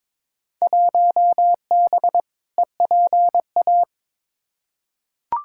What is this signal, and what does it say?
Morse code, heard as a keyed tone, about 22 wpm, 700 Hz. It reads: 1B EPA
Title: Field Day Exchanges Encore - 22wpm